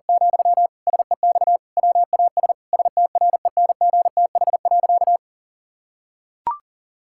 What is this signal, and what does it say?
Morse code, heard as a keyed tone, about 40 wpm, 700 Hz. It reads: , SEX WAS STRENGTH.
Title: Sentences from Top 1600 Words 40wpm